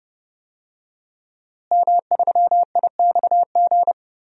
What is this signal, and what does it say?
Morse code, heard as a keyed tone, about 30 wpm, 700 Hz. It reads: M3IXG
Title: DXCC-100 Call Signs - 5 Characters - Repeated 3x - 30wpm